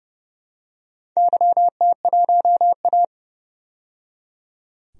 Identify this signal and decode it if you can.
Morse code, heard as a keyed tone, about 30 wpm, 700 Hz. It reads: YT1A
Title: DXCC-100 Call Signs - 4 Characters - 30wpm